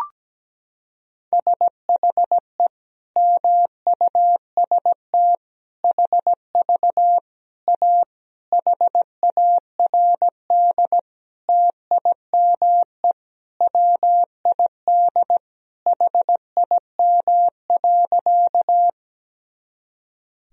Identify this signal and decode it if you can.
Morse code, heard as a keyed tone, about 17 wpm, 700 Hz. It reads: SHE MUST HV A HARD TIME WID HIM.
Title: Sentences from Top 300 Words and Abbreviations 17wpm